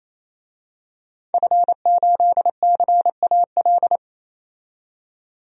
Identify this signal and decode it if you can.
Morse code, heard as a keyed tone, about 28 wpm, 700 Hz. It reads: F8CAL
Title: DXCC-100 Call Signs - 5 Characters - 28wpm